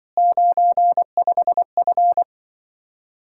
Morse code, heard as a keyed tone, about 24 wpm, 700 Hz. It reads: 95F
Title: Speed-Racing Common QSO Phrases - 20wpm